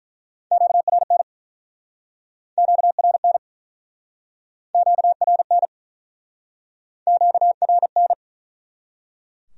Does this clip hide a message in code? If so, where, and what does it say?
Yes: Morse code from 0.5 to 8.1 s, heard as a keyed tone, about 50 wpm, 700 Hz. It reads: QRN QRN QRN QRN